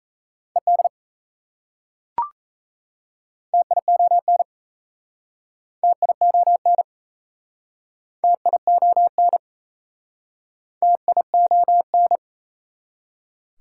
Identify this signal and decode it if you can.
Morse code, heard as a keyed tone, about 37 wpm, 700 Hz. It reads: ED TION TION TION TION